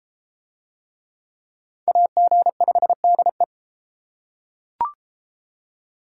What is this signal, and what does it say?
Morse code, heard as a keyed tone, about 30 wpm, 700 Hz. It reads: AG5DE